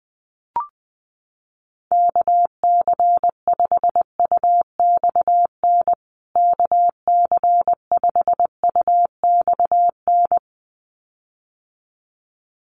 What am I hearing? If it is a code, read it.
Morse code, heard as a keyed tone, about 20 wpm, 700 Hz. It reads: KC5UXN KC5UXN
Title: USA Call Signs Repeated 2x 20wpm